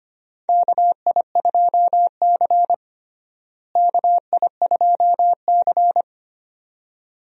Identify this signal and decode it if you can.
Morse code, heard as a keyed tone, about 25 wpm, 700 Hz. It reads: KI2C KI2C
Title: USA Call Signs - 4 Characters - Repeated 2x - 25wpm